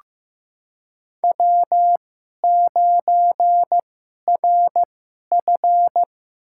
Morse code, heard as a keyed tone, about 15 wpm, spaced as for 11 wpm, 700 Hz. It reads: W9RF
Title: USA Call Signs - 4 Characters - 2x Character Spacing - 15wpm